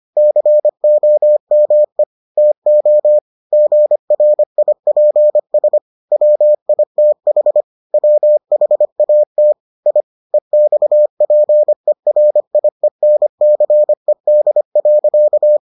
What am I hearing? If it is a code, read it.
Morse code, heard as a keyed tone, about 25 wpm, 600 Hz. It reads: COME TO GRIPS WITH WHAT I EXPERIENCED.